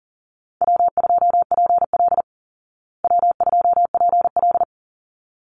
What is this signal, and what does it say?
Morse code, heard as a keyed tone, about 40 wpm, 700 Hz. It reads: W2PL W2PL
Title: Call Signs Repeated 3x - 40wpm